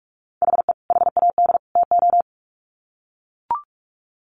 Morse code, heard as a keyed tone, about 44 wpm, 700 Hz. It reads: HE HAD TO
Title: Sets of 3 Words 45wpm